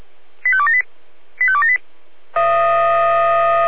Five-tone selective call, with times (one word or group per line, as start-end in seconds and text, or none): 0.4-0.8 85278
1.4-1.8 85278